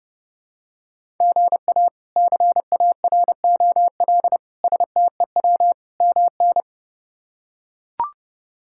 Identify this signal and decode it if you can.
Morse code, heard as a keyed tone, about 30 wpm, 700 Hz. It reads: GA CAROL STEW MN
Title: K1USN SST QSO Practice - 30wpm